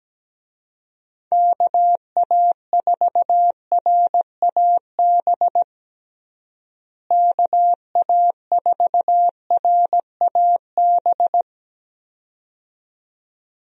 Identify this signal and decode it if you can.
Morse code, heard as a keyed tone, about 17 wpm, 700 Hz. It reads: KA4RAB KA4RAB
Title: USA Call Signs - 6 Characters - Repeated 2x - 17wpm